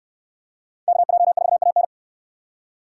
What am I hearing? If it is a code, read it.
Morse code, heard as a keyed tone, about 69 wpm, 700 Hz. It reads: KJ4AN